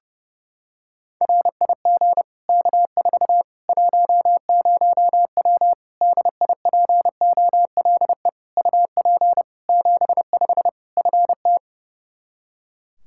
Morse code, heard as a keyed tone, about 30 wpm, 700 Hz. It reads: RIG K4 10W DIPOLE UP 75 FT
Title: Practice QSO - Exchange 2 - Info - 30wpm